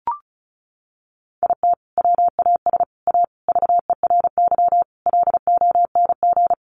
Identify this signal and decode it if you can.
Morse code, heard as a keyed tone, about 34 wpm, 700 Hz. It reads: IT WAS A VERY LONG
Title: Sentences from Top 1200 Words 35wpm